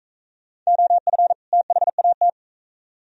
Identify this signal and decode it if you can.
Morse code, heard as a keyed tone, about 40 wpm, 700 Hz. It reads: OF THAT